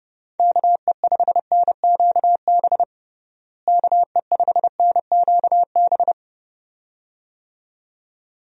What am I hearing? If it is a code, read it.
Morse code, heard as a keyed tone, about 30 wpm, 700 Hz. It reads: KE5NQB KE5NQB
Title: USA Call Signs - 6 Characters - Repeated 2x - 30wpm